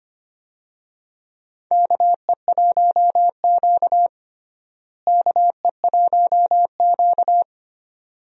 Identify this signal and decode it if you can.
Morse code, heard as a keyed tone, about 25 wpm, 700 Hz. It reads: KE1Q KE1Q
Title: USA Call Signs - 4 Characters - Repeated 2x - 25wpm